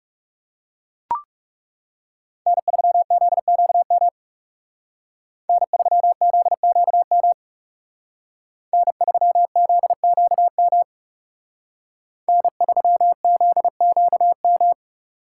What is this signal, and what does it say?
Morse code, heard as a keyed tone, about 37 wpm, 700 Hz. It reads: N3ZQM N3ZQM N3ZQM N3ZQM
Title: Speed Racing - USA 1x3 Call Signs 30wpm